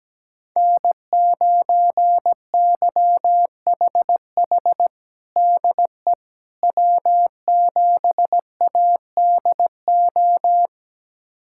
Morse code, heard as a keyed tone, about 17 wpm, 700 Hz. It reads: N9YHH DE W7ADO
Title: USA Callsign DE Callsign - 5 Characters - 17wpm